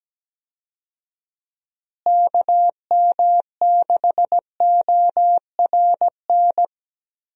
Morse code, heard as a keyed tone, about 17 wpm, 700 Hz. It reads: KM6ORN
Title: USA Call Signs - 6 Characters - Repeated 2x - 17wpm